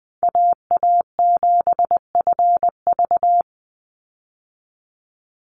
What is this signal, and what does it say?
Morse code, heard as a keyed tone, about 20 wpm, 700 Hz. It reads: AA7FV
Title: Call Signs Repeated 3x - 20wpm